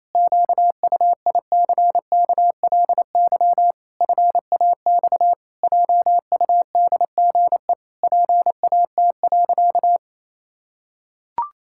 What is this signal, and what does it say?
Morse code, heard as a keyed tone, about 28 wpm, 700 Hz. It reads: QUICKLY FAX JUDGE PAT.
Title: Pangram Challenge v2 - 28wpm